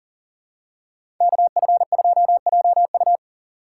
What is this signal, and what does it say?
Morse code, heard as a keyed tone, about 40 wpm, 700 Hz. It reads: KF2JU
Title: USA Call Signs - 5 Characters - 6x Character Spacing - 40wpm